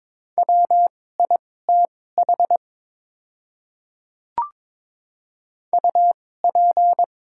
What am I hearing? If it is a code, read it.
Morse code, heard as a keyed tone, about 22 wpm, spaced as for 16 wpm, 700 Hz. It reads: WITH UP
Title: 2x Character Spacing - Top 100 Words - 22wpm